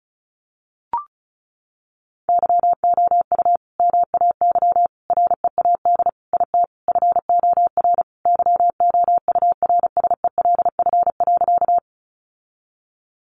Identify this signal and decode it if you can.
Morse code, heard as a keyed tone, about 34 wpm, 700 Hz. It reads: YOU MAY READ IT FOR YOURSELF.
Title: Sentences from Top 600 Words 35wpm